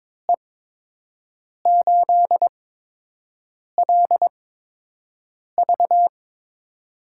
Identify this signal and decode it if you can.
Morse code, heard as a keyed tone, about 22 wpm, spaced as for 6 wpm, 700 Hz. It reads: E8LV
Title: USA Call Signs - 5 Characters - 8x Character Spacing - 22wpm